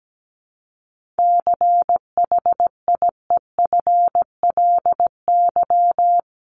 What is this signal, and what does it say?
Morse code, heard as a keyed tone, about 17 wpm, 700 Hz. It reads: CHIEFLY
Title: LY Suffix - Top 100 Words with Base Words 17wpm